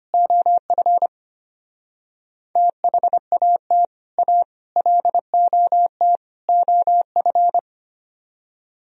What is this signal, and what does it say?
Morse code, heard as a keyed tone, about 30 wpm, 700 Hz. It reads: OF THAT A LOT OF
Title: Speed-Racing - Sets of 4 Words 25wpm